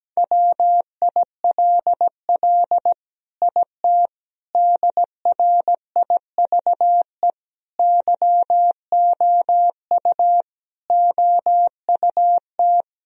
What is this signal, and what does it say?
Morse code, heard as a keyed tone, about 17 wpm, 700 Hz. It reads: WILL IT DRIVE YOU OUT
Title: Sentences from Top 1000 Words - Review Part 2 - 17wpm